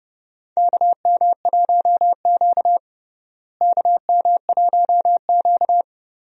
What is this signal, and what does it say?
Morse code, heard as a keyed tone, about 30 wpm, 700 Hz. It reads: KM1Q KM1Q
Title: USA Call Signs - 4 Characters - Repeated 3x - 30wpm